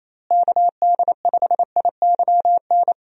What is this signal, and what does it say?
Morse code, heard as a keyed tone, about 28 wpm, 700 Hz. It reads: KD5IYN
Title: USA Call Signs - 6 Characters - Repeated 3x - 28wpm